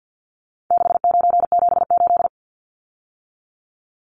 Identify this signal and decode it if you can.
Morse code, heard as a keyed tone, about 50 wpm, 700 Hz. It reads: 6978
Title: Numbers - 4-Digit 50wpm